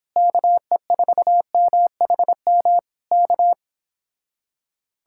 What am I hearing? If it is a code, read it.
Morse code, heard as a keyed tone, about 26 wpm, 700 Hz. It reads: KE4MHM K
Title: Speed-Racing Common QSO Phrases Encore - 22wpm